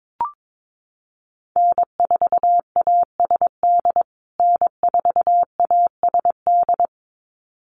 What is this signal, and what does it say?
Morse code, heard as a keyed tone, about 22 wpm, 700 Hz. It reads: N4ASD N4ASD
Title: USA Call Signs Repeated 2x 22wpm